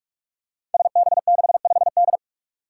Morse code, heard as a keyed tone, about 45 wpm, 700 Hz. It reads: SB65D
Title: Five Challenging Letter-Number Combinations 45wpm